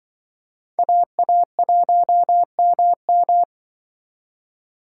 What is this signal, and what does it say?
Morse code, heard as a keyed tone, about 24 wpm, 700 Hz. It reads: AA1MM
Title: Speed Racing - USA 2x2 Call Signs 20wpm